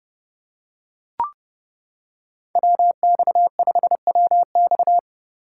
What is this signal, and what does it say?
Morse code, heard as a keyed tone, about 28 wpm, 700 Hz. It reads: WX5WX